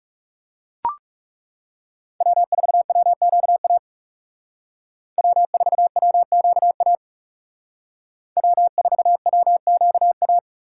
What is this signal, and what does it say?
Morse code, heard as a keyed tone, about 39 wpm, 700 Hz. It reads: W4WQA W4WQA W4WQA